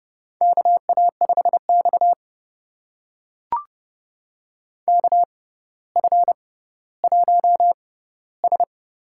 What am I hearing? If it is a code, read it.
Morse code, heard as a keyed tone, about 30 wpm, spaced as for 15 wpm, 700 Hz. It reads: KA5X KF1S
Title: USA Call Signs - 4 Characters - 6x Character Spacing - 30wpm